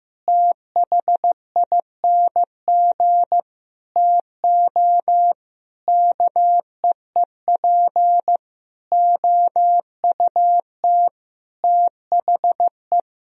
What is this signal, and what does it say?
Morse code, heard as a keyed tone, about 15 wpm, 700 Hz. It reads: THING TO KEEP OUT THE
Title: Sentences from Top 1000 Words - Review Part 2 - 15wpm